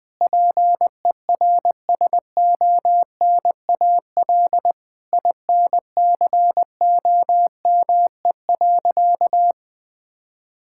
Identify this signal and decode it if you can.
Morse code, heard as a keyed tone, about 20 wpm, 700 Hz. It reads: PERSONAL INCOME.
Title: Sentences from Top 1000 Words - Review Part 5 - 20wpm